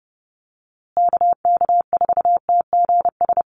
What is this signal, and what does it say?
Morse code, heard as a keyed tone, about 30 wpm, 700 Hz. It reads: KK4TGS
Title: USA Call Signs Repeated 3x 30wpm